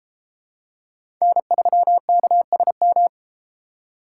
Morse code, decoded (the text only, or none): N3KSM